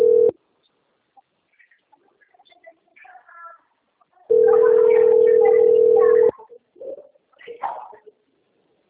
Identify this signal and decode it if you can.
Ringback tone (North America), 440 + 480 Hz.
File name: out- - - - - .66311